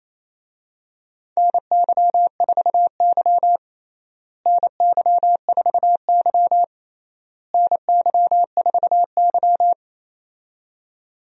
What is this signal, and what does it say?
Morse code, heard as a keyed tone, about 28 wpm, 700 Hz. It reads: NY4Y NY4Y NY4Y
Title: USA Call Signs - 4 Characters - Repeated 3x - 28wpm